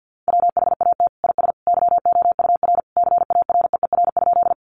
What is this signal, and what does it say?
Morse code, heard as a keyed tone, about 50 wpm, 700 Hz. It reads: WHAT IS YOUR CAREER?
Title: Sentences from Top 1000 Words 50wpm